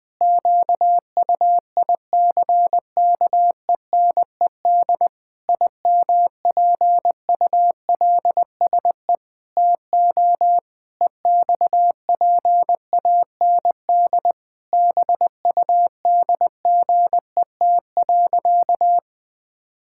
Morse code, heard as a keyed tone, about 20 wpm, 700 Hz. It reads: QUICKENED IMPULSE TO EXPAND BUDGET.
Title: Pangram Challenge v2 - 20wpm